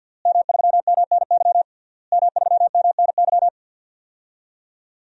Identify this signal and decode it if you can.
Morse code, heard as a keyed tone, about 46 wpm, 700 Hz. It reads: M3MNY M3MNY
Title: DXCC-100 Call Signs - 5 Characters - Repeated 3x - 50wpm